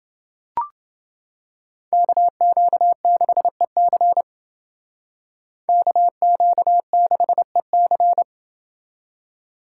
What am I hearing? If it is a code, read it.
Morse code, heard as a keyed tone, about 28 wpm, 700 Hz. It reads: KQ6EC KQ6EC